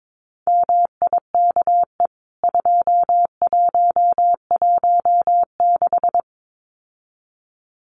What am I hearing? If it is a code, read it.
Morse code, heard as a keyed tone, about 22 wpm, 700 Hz. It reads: MIKE 2116
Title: CWT Exchange 22wpm